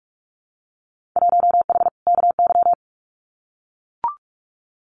Morse code, heard as a keyed tone, about 44 wpm, 700 Hz. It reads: 1H KY
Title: Winter Field Day Exchanges 45wpm